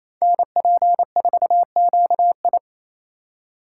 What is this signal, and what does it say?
Morse code, heard as a keyed tone, about 28 wpm, 700 Hz. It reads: NP4QI
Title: USA Call Signs - 2x Character Spacing - 28wpm